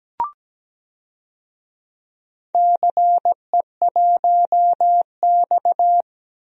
Morse code, heard as a keyed tone, about 17 wpm, 700 Hz. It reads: CE1X
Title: DXCC-100 Call Signs - 4 Characters - 17wpm